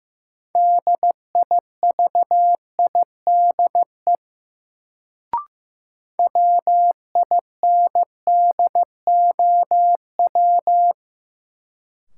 Morse code, heard as a keyed tone, about 15 wpm, 700 Hz. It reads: DIVIDE WINDOW
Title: Sentences from Top 1600 Words - New Word Drill Alt - 15wpm